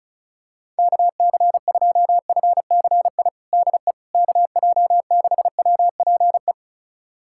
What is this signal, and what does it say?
Morse code, heard as a keyed tone, about 36 wpm, 700 Hz. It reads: KC2FCI DE KJ6WPE
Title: USA Callsign DE Callsign 35wpm